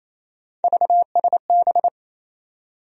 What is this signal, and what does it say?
Morse code, heard as a keyed tone, about 27 wpm, 700 Hz. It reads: VSB
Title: USA Call Sign Postfixes - 3 Letters 28wpm